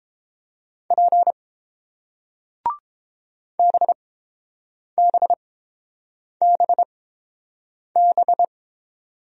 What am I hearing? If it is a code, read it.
Morse code, heard as a keyed tone, about 30 wpm, spaced as for 7 wpm, 700 Hz. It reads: P BBBB